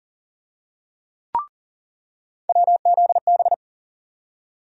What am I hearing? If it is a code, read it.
Morse code, heard as a keyed tone, about 40 wpm, 700 Hz. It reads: WZB
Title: USA Call Sign Postfixes - 3 Letters 40wpm